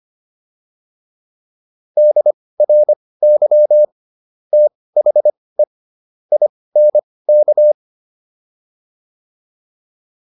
Morse code, heard as a keyed tone, about 25 wpm, spaced as for 18 wpm, 600 Hz. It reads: DRY THE INK